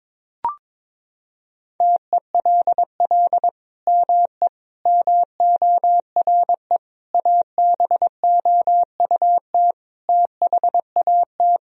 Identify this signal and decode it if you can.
Morse code, heard as a keyed tone, about 21 wpm, 700 Hz. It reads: TELL ME MORE ABOUT THAT
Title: Sentences from Top 1000 Words - Review Part 4 - 22wpm